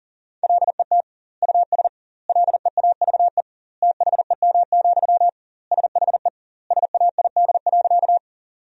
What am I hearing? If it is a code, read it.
Morse code, heard as a keyed tone, about 40 wpm, 700 Hz. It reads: LET US LEAVE THEM, SHE SAID.
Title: 2x Word-Spacing - Sentences from Top 400 Words - 40wpm